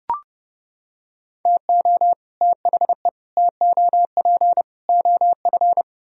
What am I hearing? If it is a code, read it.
Morse code, heard as a keyed tone, about 29 wpm, 700 Hz. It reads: TO THE TOP OF